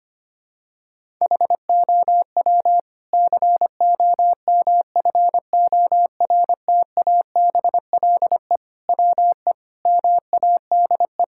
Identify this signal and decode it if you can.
Morse code, heard as a keyed tone, about 25 wpm, 700 Hz. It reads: HOW COMFORTABLE WE MADE
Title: Sentences from Top 1600 Words 25wpm